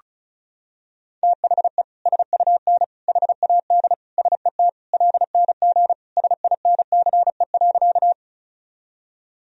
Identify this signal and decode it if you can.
Morse code, heard as a keyed tone, about 34 wpm, 700 Hz. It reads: THE SUN HAD SET LNG SINCE.
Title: Sentences from Top 300 Words and Abbreviations 35wpm